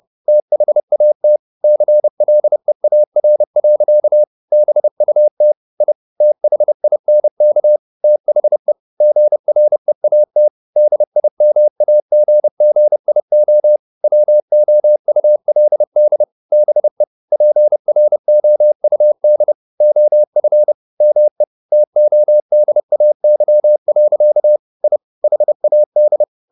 Morse code, heard as a keyed tone, about 30 wpm, 600 Hz. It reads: THAT CLEAR. BUT I THINK THE GREAT DIMAGGIO WOULD BE PROUD OF ME TODAY. I HAD